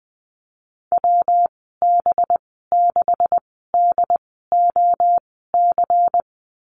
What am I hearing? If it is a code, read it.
Morse code, heard as a keyed tone, about 20 wpm, spaced as for 15 wpm, 700 Hz. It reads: WB6DOC
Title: 200 Lesson 32g - Letter k - Call Signs 20wpm